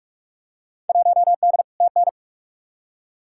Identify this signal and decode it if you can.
Morse code, heard as a keyed tone, about 41 wpm, 700 Hz. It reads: 1D TN
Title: Field Day Exchanges - 45wpm